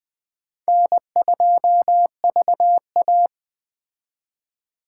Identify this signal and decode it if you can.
Morse code, heard as a keyed tone, about 19 wpm, 700 Hz. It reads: N2VA